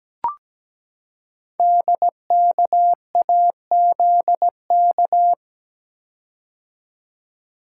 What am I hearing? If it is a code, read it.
Morse code, heard as a keyed tone, about 17 wpm, 700 Hz. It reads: DKAZK
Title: Code Groups - 5 Letters - 17wpm